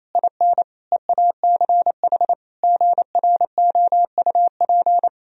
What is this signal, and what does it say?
Morse code, heard as a keyed tone, about 28 wpm, 700 Hz. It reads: IN EACH GROUP
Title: Sentences from Top 1000 Words - Review Part 1 - 28wpm